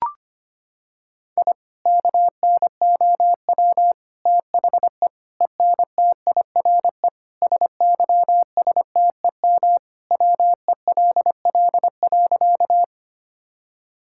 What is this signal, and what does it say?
Morse code, heard as a keyed tone, about 25 wpm, 700 Hz. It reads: I KNOW THE ENTIRE SYSTEM WELL.
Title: Sentences from Top 1000 Words - Review Part 5 - 25wpm